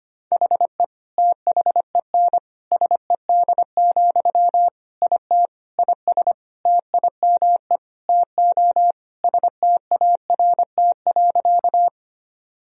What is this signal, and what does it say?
Morse code, heard as a keyed tone, about 25 wpm, 700 Hz. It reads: HE THEN SED, IT IS TIME TO START.
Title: Sentences from Top 300 Words and Abbreviations 25wpm